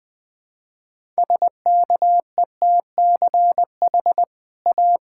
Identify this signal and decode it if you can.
Morse code, heard as a keyed tone, about 20 wpm, 700 Hz. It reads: SKETCH A
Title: Easy Sentences with Hints - 20wpm